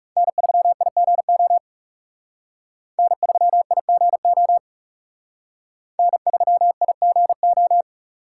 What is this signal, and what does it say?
Morse code, heard as a keyed tone, about 38 wpm, 700 Hz. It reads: N3IGO N3IGO N3IGO